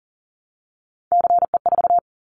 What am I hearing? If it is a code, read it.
Morse code, heard as a keyed tone, about 40 wpm, 700 Hz. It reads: CE4
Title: Callsign Prefixes for Top 100 DXCC - 40wpm